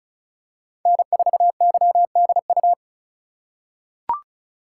Morse code, heard as a keyed tone, about 34 wpm, 700 Hz. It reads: N4YDU
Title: K1USN SST QSO Practice - 35wpm